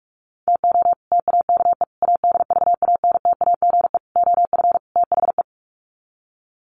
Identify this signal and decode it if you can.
Morse code, heard as a keyed tone, about 44 wpm, 700 Hz. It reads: TO TAKE ADVANTAGE OF THE
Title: Sets of 5 Words 45wpm